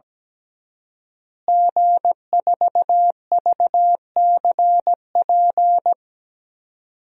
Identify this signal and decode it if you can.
Morse code, heard as a keyed tone, about 17 wpm, 700 Hz. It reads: G4VCP
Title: DXCC-100 Call Signs - 5 Characters - Repeated 2x - 17wpm